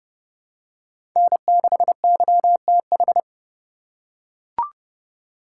Morse code, heard as a keyed tone, about 30 wpm, 700 Hz. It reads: N6YTH
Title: USA Call Signs - 5 Characters - 4x Character Spacing - 30wpm